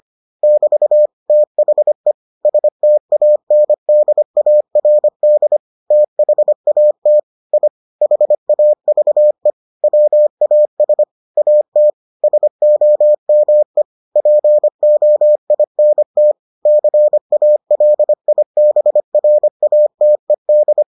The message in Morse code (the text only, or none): <BT> THE STANDARD THAT I HAVE WAS AT SOME POINT CALIBRATED